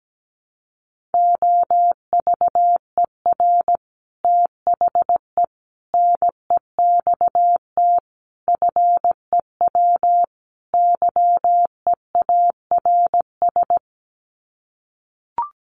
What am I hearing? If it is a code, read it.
Morse code, heard as a keyed tone, about 17 wpm, 700 Hz. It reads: OVER THE NEXT FEW YEARS
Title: Sets of 5 Words Encore 17wpm